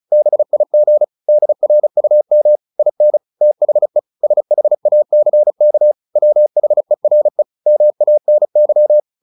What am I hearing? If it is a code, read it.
Morse code, heard as a keyed tone, about 34 wpm, 600 Hz. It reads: BIG DRUM IN THE SHACK WHERE MANY
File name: the old man and the sea / the old man and the sea 1 35wpm